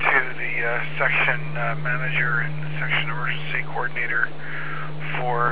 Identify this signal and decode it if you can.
Emergency Alert System (SAME) tones, heard as harsh buzzing bursts, with speech. It is heard from 0.4 to 4.3 s.